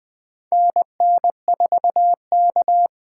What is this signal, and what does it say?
Morse code, heard as a keyed tone, about 20 wpm, 700 Hz. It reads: NN4K
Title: USA Call Signs - 4 Characters - 3x Character Spacing - 20wpm